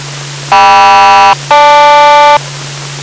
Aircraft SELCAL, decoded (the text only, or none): KM-GL